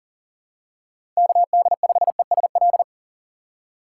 Morse code, heard as a keyed tone, about 40 wpm, 700 Hz. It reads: KD5ESL
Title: USA Call Signs - 6 Characters - 4x Character Spacing - 40wpm